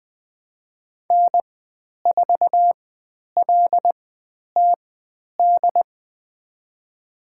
Morse code, heard as a keyed tone, about 20 wpm, spaced as for 10 wpm, 700 Hz. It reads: N4LTD
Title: 102 Lesson 17g - Letter u - Call Signs 20wpm